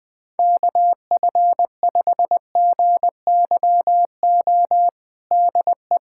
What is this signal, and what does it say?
Morse code, heard as a keyed tone, about 21 wpm, 700 Hz. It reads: KF5GYO DE
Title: USA Callsign DE Callsign 20wpm